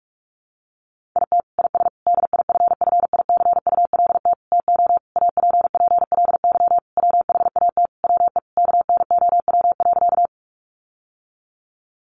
Morse code, heard as a keyed tone, about 44 wpm, 700 Hz. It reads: IT IS DIFFICULT TO APPLY WHAT WE KNOW.
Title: Sentences from Top 1200 Words 45wpm